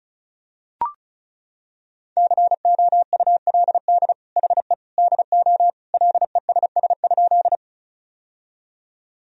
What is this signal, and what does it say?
Morse code, heard as a keyed tone, about 36 wpm, 700 Hz. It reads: COULD HE DO LESS?
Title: Sentences from Top 1000 Words - Review Part 2 - 35wpm